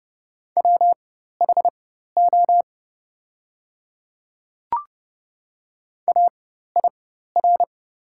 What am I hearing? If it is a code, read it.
Morse code, heard as a keyed tone, about 31 wpm, spaced as for 14 wpm, 700 Hz. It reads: WHO AIR
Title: 4x Character Spacing - Top 200 Words - 30wpm